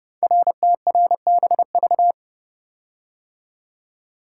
Morse code, heard as a keyed tone, about 28 wpm, 700 Hz. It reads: RTRBV
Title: Code Groups - 5 Letters - 30wpm